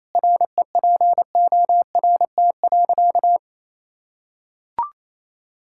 Morse code, heard as a keyed tone, about 28 wpm, 700 Hz. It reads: REPORT.
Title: Sentences from Top 1000 Words - Review Part 6 - 28wpm